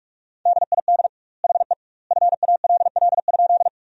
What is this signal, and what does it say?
Morse code, heard as a keyed tone, about 48 wpm, 700 Hz. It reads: DID HE FALL?
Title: 2x Word-Spacing - Sentences from Top 500 Words - 45wpm